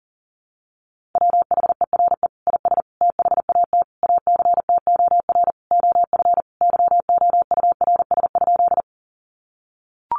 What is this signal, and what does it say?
Morse code, heard as a keyed tone, about 40 wpm, 700 Hz. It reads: WHERE IS THAT ACTOR OF YOURS?
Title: Sentences from Top 1500 Words 40wpm